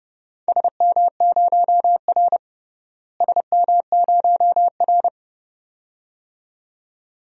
Morse code, heard as a keyed tone, about 31 wpm, 700 Hz. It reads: SM0R SM0R
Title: DXCC-100 Call Signs - 4 Characters - Repeated 2x - 30wpm